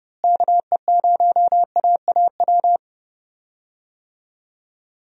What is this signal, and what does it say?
Morse code, heard as a keyed tone, about 30 wpm, 700 Hz. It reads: KE0AAW
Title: USA Call Signs - 6 Characters - Repeated 2x - 30wpm